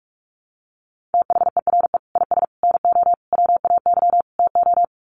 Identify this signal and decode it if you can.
Morse code, heard as a keyed tone, about 44 wpm, 700 Hz. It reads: THERE IS NO WAY TO
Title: Sets of 5 Words Encore 45wpm